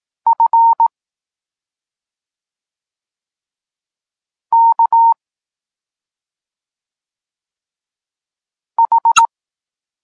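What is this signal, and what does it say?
Morse code, heard as a keyed tone, about 18 wpm, 920 Hz. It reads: F K H
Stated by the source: Polish Morse lesson teaching the letter C, from SP9MOA Niepołomicki Klub Krótkofalowców.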